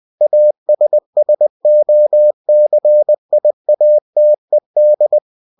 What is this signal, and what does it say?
Morse code, heard as a keyed tone, about 20 wpm, 600 Hz. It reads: ASSOCIATED
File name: cw-books / winnie-the-pooh / winnie-20wpm-page-0114